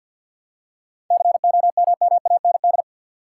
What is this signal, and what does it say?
Morse code, heard as a keyed tone, about 52 wpm, 700 Hz. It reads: COMMAND
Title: Seven Letter Words - 50wpm